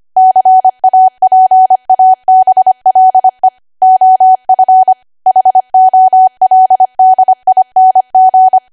Morse code, heard as a keyed tone, about 25 wpm, 750 Hz. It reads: CAPABLE OF HOLDING